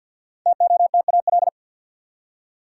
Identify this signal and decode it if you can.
Morse code, heard as a keyed tone, about 50 wpm, 700 Hz. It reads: TOTAL
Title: Five Letter Words - 50wpm